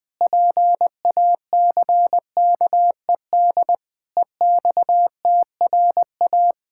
Morse code, heard as a keyed tone, about 20 wpm, 700 Hz. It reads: PACKED EXTRA
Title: Pangram Challenge v2 - 20wpm